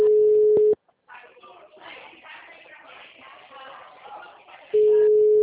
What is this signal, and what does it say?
Ringback tone (Europe, 425 Hz).
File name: out- -2020- - - .18475